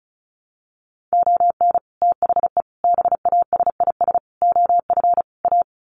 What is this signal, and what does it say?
Morse code, heard as a keyed tone, about 34 wpm, 700 Hz. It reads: ON THE BASIS OF A
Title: Sets of 5 Words Encore 35wpm